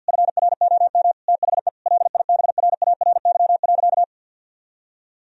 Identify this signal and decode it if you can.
Morse code, heard as a keyed tone, about 50 wpm, 700 Hz. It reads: FROM THE LIBRARY.
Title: Sentences from Top 1000 Words - Review Part 4 - 50wpm